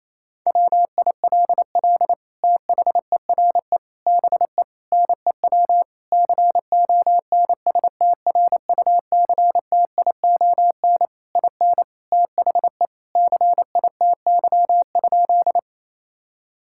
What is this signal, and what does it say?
Morse code, heard as a keyed tone, about 28 wpm, 700 Hz. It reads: WILL THERE BE NEW CONSTRUCTION IN THE CITY?
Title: Sentences from Top 1600 Words 28wpm